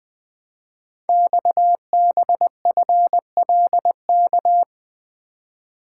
Morse code, heard as a keyed tone, about 20 wpm, 700 Hz. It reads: XBFLK
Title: Code Groups - 5 Letters - 20wpm